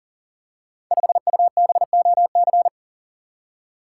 Morse code, heard as a keyed tone, about 38 wpm, 690 Hz. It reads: 5UBOC
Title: Code Groups - 5 Letter:Numbers - 40wpm